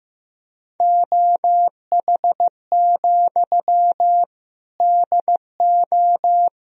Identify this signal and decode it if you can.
Morse code, heard as a keyed tone, about 15 wpm, 700 Hz. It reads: OH, DO
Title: Sentences from Top 500 Words and Abbreviations 15wpm